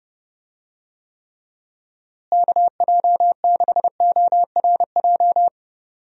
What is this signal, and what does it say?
Morse code, heard as a keyed tone, about 29 wpm, 700 Hz. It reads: KJ6ORJ
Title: USA Call Signs - 6 Characters - Repeated 2x - 30wpm